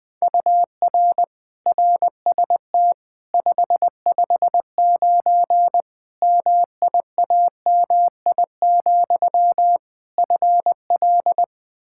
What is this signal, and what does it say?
Morse code, heard as a keyed tone, about 20 wpm, 700 Hz. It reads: UR RST 559 MIAMI, FL
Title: K1USN SST QSO Practice 20wpm